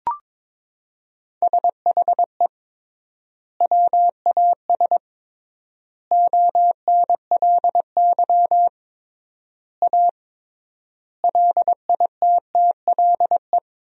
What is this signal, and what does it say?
Morse code, heard as a keyed tone, about 22 wpm, 700 Hz. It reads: SHE WAS ONLY A LITTLE
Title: 3x Word-Spacing - Sentences from Top 400 Words - 22wpm